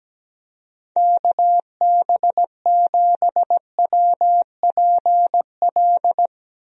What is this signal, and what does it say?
Morse code, heard as a keyed tone, about 17 wpm, 700 Hz. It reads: KB7WPL
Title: USA Call Signs - 6 Characters - Repeated 2x - 17wpm